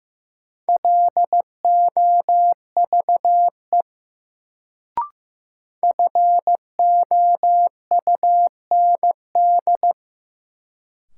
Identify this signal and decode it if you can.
Morse code, heard as a keyed tone, about 15 wpm, 700 Hz. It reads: LOVE FOUND
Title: Sets of 5 Words Encore - New Word Drill - 15wpm